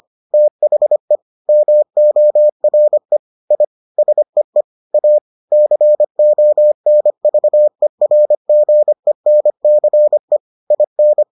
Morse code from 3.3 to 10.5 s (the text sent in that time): I SEE A CONVERGENCE